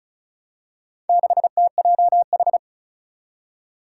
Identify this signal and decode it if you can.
Morse code, heard as a keyed tone, about 36 wpm, 700 Hz. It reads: 6TJH
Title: Four Challenging Letter-Number Combinations 35wpm